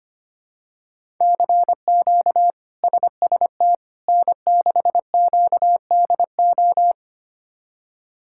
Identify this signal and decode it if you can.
Morse code, heard as a keyed tone, about 25 wpm, 700 Hz. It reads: CQ SST N6QDO
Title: K1USN SST QSO Practice - 25wpm